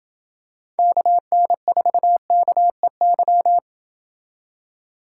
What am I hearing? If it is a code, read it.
Morse code, heard as a keyed tone, about 28 wpm, 700 Hz. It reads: KN4KEY